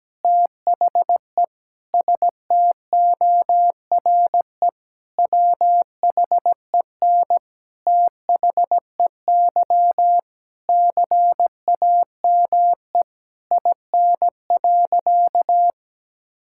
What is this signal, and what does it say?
Morse code, heard as a keyed tone, about 17 wpm, 700 Hz. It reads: THE STORE WHEN THEY CAME IN.
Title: Sentences from Top 1000 Words - Review Part 5 - 17wpm